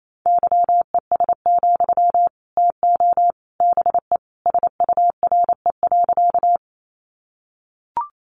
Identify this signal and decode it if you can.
Morse code, heard as a keyed tone, about 28 wpm, 700 Hz. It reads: YES, TO BE SURE.
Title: Sentences from Top 500 Words 28wpm